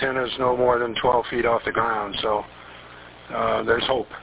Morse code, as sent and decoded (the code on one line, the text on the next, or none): none